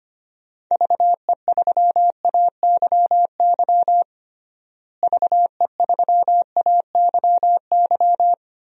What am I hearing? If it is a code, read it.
Morse code, heard as a keyed tone, about 25 wpm, 700 Hz. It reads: VE3AYY VE3AYY
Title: DXCC-100 Call Signs - 6 Characters - Repeated 3x - 25wpm